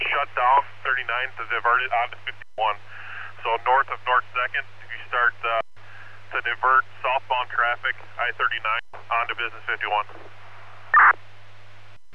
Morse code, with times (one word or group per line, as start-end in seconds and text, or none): none